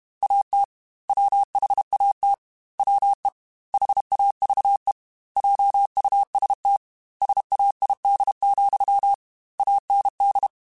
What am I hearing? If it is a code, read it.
Morse code, heard as a keyed tone, about 32 wpm, 800 Hz. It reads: AT WHAT WE HAVE JUST SAID, AND
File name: Technical Subjects / Technical Subjects-mp3s / OR Gate @30